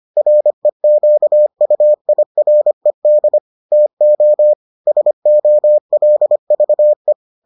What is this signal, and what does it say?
Morse code, heard as a keyed tone, about 25 wpm, 600 Hz. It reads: REQUIRED TO SOLVE